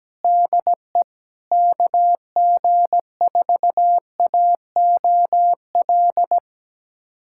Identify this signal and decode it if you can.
Morse code, heard as a keyed tone, about 17 wpm, 700 Hz. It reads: DE KG4AOL
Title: USA Callsign DE Callsign - 6 Characters - 17wpm